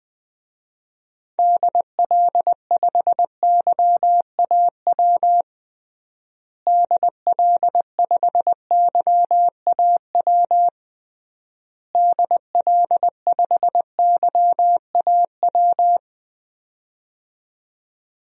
Morse code, heard as a keyed tone, about 20 wpm, 700 Hz. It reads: DL5YAW DL5YAW DL5YAW
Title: DXCC-100 Call Signs - 6 Characters - Repeated 3x - 20wpm